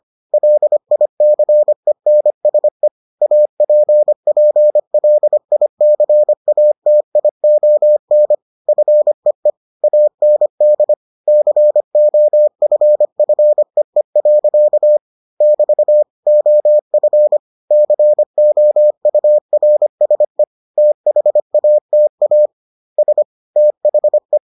Morse code, heard as a keyed tone, about 25 wpm, 600 Hz. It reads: LICENSE APPLICATION FEE AND COFFEE. <BT> OF COURSE THATA S THE